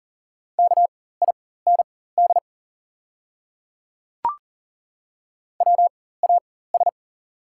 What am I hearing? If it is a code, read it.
Morse code, heard as a keyed tone, about 38 wpm, spaced as for 18 wpm, 690 Hz. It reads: KIND WAS
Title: 4x Character Spacing - Top 300 Words - 40wpm